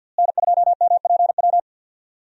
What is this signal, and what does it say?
Morse code, heard as a keyed tone, about 48 wpm, 700 Hz. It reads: N2MPW
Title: USA Call Signs - 5 Characters - Repeated 3x - 50wpm